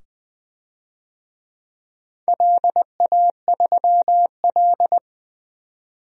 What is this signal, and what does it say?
Morse code, heard as a keyed tone, about 20 wpm, 700 Hz. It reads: LA3L
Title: DXCC-100 Call Signs - 4 Characters - 20wpm